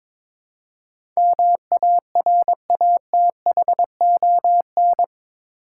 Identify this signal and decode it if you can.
Morse code, heard as a keyed tone, about 21 wpm, 700 Hz. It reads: MARATHON
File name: practice / 22wpm / Eight Letter Words - 22wpm